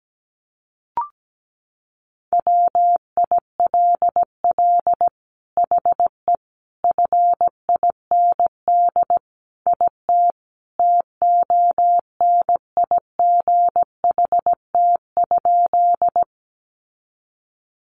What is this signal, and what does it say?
Morse code, heard as a keyed tone, about 17 wpm, 700 Hz. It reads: WILL HE FIND IT TONIGHT?
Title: Sentences from Top 600 Words 17wpm